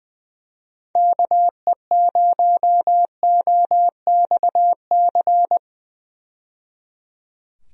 Morse code, heard as a keyed tone, about 20 wpm, 700 Hz. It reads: KE0OXC
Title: USA Call Signs - 6 Characters - 20wpm